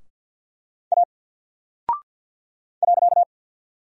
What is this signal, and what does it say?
Morse code, heard as a keyed tone, about 52 wpm, 700 Hz. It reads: A .
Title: Single Character (Letter:Number:Punctuation:Prosign) Alt 50wpm